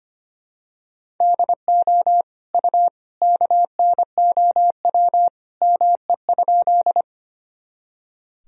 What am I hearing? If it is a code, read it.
Morse code, heard as a keyed tone, about 25 wpm, 700 Hz. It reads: DO U KNOW ME?
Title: Sentences from Top 200 Words and Abbreviations 25wpm